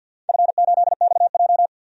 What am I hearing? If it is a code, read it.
Morse code, heard as a keyed tone, about 46 wpm, 700 Hz. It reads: F8XJ
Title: DXCC-100 Call Signs - 4 Characters - 50wpm